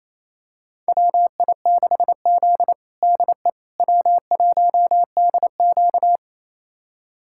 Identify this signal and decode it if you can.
Morse code, heard as a keyed tone, about 28 wpm, 700 Hz. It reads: WI6Z DE W1DQ
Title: USA Callsign DE Callsign - 4 Characters - 28wpm